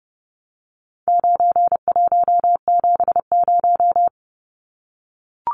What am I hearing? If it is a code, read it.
Morse code, heard as a keyed tone, about 30 wpm, 700 Hz. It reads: 9170
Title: Numbers - 4-Digit 30wpm